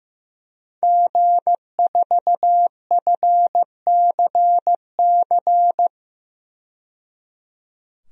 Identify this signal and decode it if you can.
Morse code, heard as a keyed tone, about 15 wpm, 700 Hz. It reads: G4FCC
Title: DXCC-100 Call Signs - 5 Characters - 15wpm